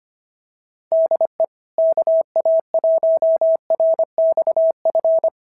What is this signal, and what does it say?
Morse code, heard as a keyed tone, about 24 wpm, 650 Hz. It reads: DE KA1RXF
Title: SOTA QSO Practice 25wpm